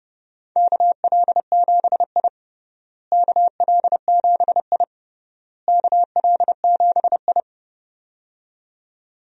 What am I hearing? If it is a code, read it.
Morse code, heard as a keyed tone, about 32 wpm, 700 Hz. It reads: KL7I KL7I KL7I
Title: DXCC-100 Call Signs - 4 Characters - Repeated 3x - 30wpm